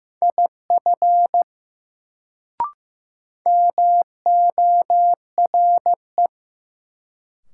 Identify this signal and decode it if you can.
Morse code, heard as a keyed tone, about 15 wpm, 700 Hz. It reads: IF MORE
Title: Sets of 2 Words - New Word Drill - 15wpm